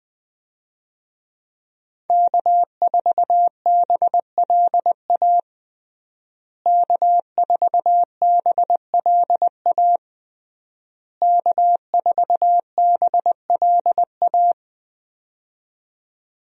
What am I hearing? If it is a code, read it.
Morse code, heard as a keyed tone, about 21 wpm, 700 Hz. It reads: K4BLA K4BLA K4BLA
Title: USA Call Signs - 5 Characters - Repeated 3x - 20wpm